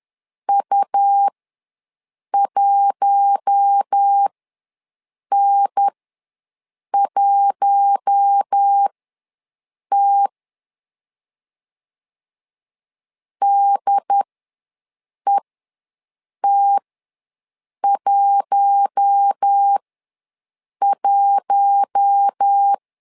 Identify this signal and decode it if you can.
Morse code, heard as a keyed tone, about 10 wpm, spaced as for 6 wpm, 800 Hz. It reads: U1N1T DET11
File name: LEZIONI / 4 Q1 ws 10wpm 2538